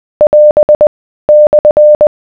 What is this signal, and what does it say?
Morse code, heard as a keyed tone, about 20 wpm, 600 Hz. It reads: <AS> /